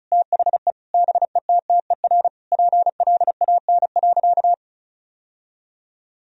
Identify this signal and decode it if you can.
Morse code, heard as a keyed tone, about 36 wpm, 700 Hz. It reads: THE BETTER PLAN.
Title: Sentences from Top 1000 Words - Review Part 2 - 35wpm